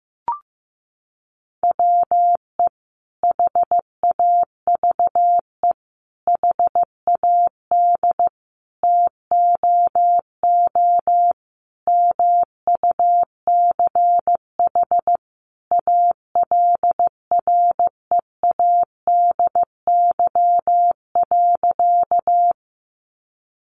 Morse code, heard as a keyed tone, about 15 wpm, 700 Hz. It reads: WE HAVE HAD TOO MUCH ALREADY.
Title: Sentences from Top 700 Words 15wpm